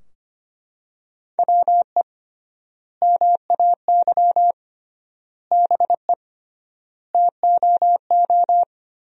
Morse code, heard as a keyed tone, about 25 wpm, 700 Hz. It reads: WE MAY BE TOO
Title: 3x Word-Spacing - Sentences from Top 300 Words - 25wpm